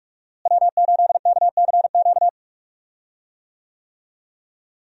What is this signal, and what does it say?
Morse code, heard as a keyed tone, about 42 wpm, 700 Hz. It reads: W8KCQ
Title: USA Call Signs - 5 Characters - Repeated 2x - 45wpm